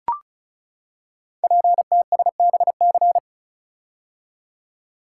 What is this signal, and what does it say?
Morse code, heard as a keyed tone, about 36 wpm, 700 Hz. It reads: PTSBC
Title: Code Groups - 5 Letters - 35wpm